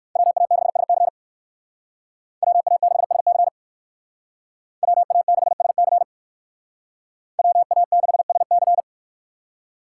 Morse code, heard as a keyed tone, about 62 wpm, 690 Hz. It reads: WA6SC WA6SC WA6SC WA6SC